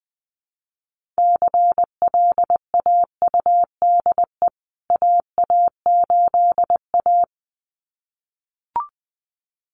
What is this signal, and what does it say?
Morse code, heard as a keyed tone, about 19 wpm, 700 Hz. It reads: CLAUDE AA8A
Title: CWT Exchange 20wpm